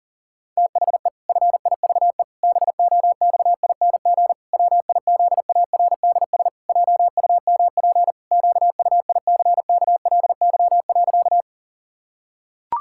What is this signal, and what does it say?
Morse code, heard as a keyed tone, about 40 wpm, 700 Hz. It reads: THE FIVE BOXING WIZARDS JUMP QUICKLY.
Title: Pangram Challenge v2 - 40wpm